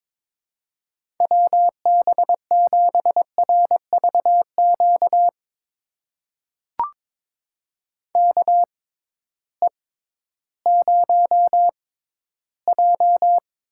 Morse code, heard as a keyed tone, about 22 wpm, spaced as for 13 wpm, 700 Hz. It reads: WB7RVQ KE0J
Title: USA Call Signs - 6 Characters - 6x Character Spacing - 22wpm